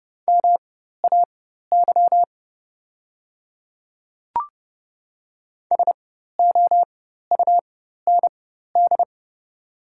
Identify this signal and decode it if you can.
Morse code, heard as a keyed tone, about 30 wpm, spaced as for 14 wpm, 700 Hz. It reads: MAY SOUND
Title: 4x Character Spacing - Top 100 Words - 30wpm